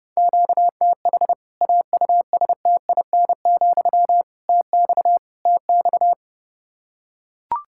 Morse code, heard as a keyed tone, about 30 wpm, 700 Hz. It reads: QTH AUSTIN, TX TX
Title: K1USN SST QSO Practice 30wpm